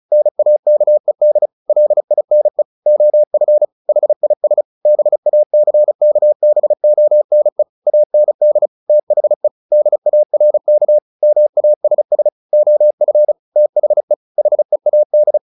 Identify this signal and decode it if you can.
Morse code, heard as a keyed tone, about 34 wpm, 600 Hz. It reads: NAKED LINE OF HIS BACKBONE AND THE DARK MASS OF THE HEAD
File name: the old man and the sea 2 35wpm